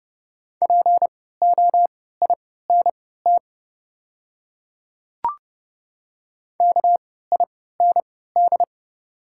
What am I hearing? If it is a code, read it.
Morse code, heard as a keyed tone, about 30 wpm, spaced as for 17 wpm, 690 Hz. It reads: POINT KIND
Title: 3x Character Spacing - Top 300 Words - 30wpm